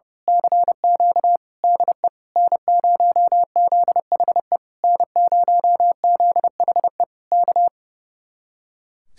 Morse code, heard as a keyed tone, about 28 wpm, 700 Hz. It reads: CQ DE N0ZHE N0ZHE K
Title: Speed-Racing Common QSO Phrases Encore - 30wpm